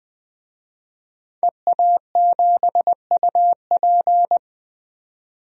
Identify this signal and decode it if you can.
Morse code, heard as a keyed tone, about 20 wpm, 700 Hz. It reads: EA7UP
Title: DXCC-100 Call Signs - 5 Characters - 20wpm